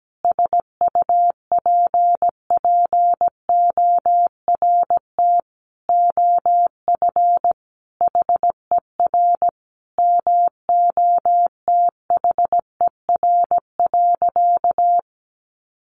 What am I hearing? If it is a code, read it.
Morse code, heard as a keyed tone, about 17 wpm, 700 Hz. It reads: SUPPORT OF HER MOTHER.
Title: Sentences from Top 800 Words 17wpm